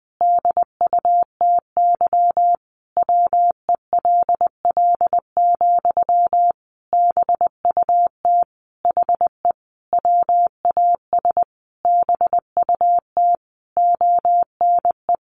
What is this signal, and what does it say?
Morse code, heard as a keyed tone, about 19 wpm, 700 Hz. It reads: DUTY WELL, BUT HE WAS BUT ONE
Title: Sentences from Top 1500 Words 20wpm